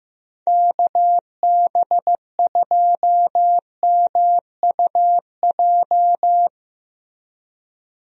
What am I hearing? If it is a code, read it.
Morse code, heard as a keyed tone, about 15 wpm, 700 Hz. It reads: KB2MUJ
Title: USA Call Signs - 6 Characters - Repeated 3x - 15wpm